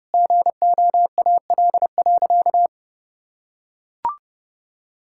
Morse code, heard as a keyed tone, about 30 wpm, 700 Hz. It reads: GOAL.
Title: Sentences from Top 1000 Words - Review Part 6 - 30wpm